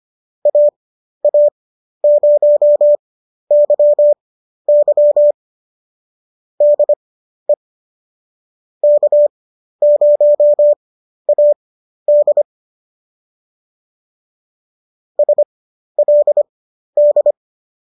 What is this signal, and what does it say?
Morse code, heard as a keyed tone, about 25 wpm, spaced as for 12 wpm, 590 Hz. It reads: AA0YY DE K0AD SLD